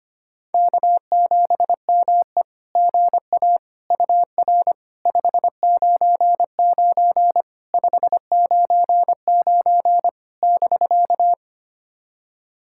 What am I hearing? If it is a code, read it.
Morse code, heard as a keyed tone, about 25 wpm, 700 Hz. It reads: K7ME GA UR 599 599 <BK>
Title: SOTA QSO Practice 25wpm